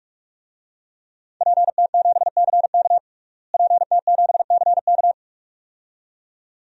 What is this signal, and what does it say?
Morse code, heard as a keyed tone, about 46 wpm, 700 Hz. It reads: PT7CK PT7CK
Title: DXCC-100 Call Signs - 5 Characters - Repeated 2x - 45wpm